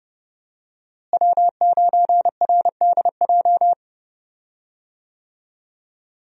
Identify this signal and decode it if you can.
Morse code, heard as a keyed tone, about 30 wpm, 700 Hz. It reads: W9RDJ
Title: USA Call Signs - 5 Characters - Repeated 2x - 30wpm